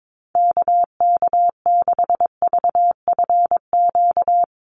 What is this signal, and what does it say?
Morse code, heard as a keyed tone, about 22 wpm, 700 Hz. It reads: KK6VFQ
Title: USA Call Signs 22wpm